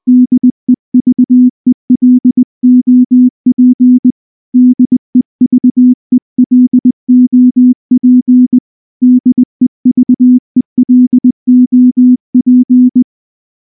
Morse code, heard as a keyed tone, about 19 wpm, 250 Hz. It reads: DEVELOP DEVELOP DEVELOP
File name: morse-training-20WPM